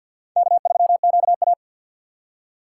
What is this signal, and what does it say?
Morse code, heard as a keyed tone, about 50 wpm, 690 Hz. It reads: K3QA
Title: USA Call Signs - 6x Character Spacing - 50wpm